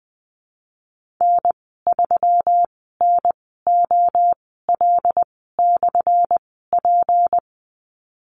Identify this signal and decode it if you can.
Morse code, heard as a keyed tone, about 20 wpm, spaced as for 15 wpm, 700 Hz. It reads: N3NOL/P
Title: 179 Lesson 29g - Slash - Call Signs 20wpm